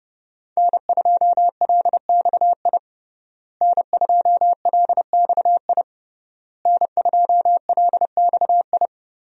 Morse code, heard as a keyed tone, about 32 wpm, 700 Hz. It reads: N2LXI N2LXI N2LXI
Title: USA Call Signs - 5 Characters - Repeated 3x - 30wpm